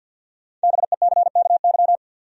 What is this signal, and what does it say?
Morse code, heard as a keyed tone, about 52 wpm, 700 Hz. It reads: BECKY
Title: USA Common Names - Max 5 Letters - 50wpm